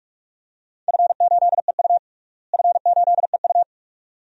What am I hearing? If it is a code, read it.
Morse code, heard as a keyed tone, about 41 wpm, 700 Hz. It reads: F8EU F8EU
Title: DXCC-100 Call Signs - 4 Characters - Repeated 2x - 45wpm